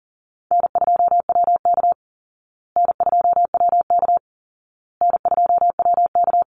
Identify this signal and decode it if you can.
Morse code, heard as a keyed tone, about 40 wpm, 700 Hz. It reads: N2WK N2WK N2WK
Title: Easy Call Signs Repeated 3x - 40wpm